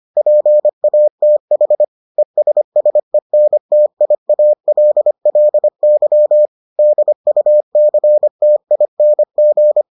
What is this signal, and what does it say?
Morse code, heard as a keyed tone, about 25 wpm, 600 Hz. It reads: PATH ESSENTIALLY DUCTING